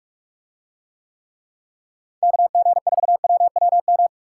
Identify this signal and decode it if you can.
Morse code, heard as a keyed tone, about 44 wpm, 700 Hz. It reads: KG4WWM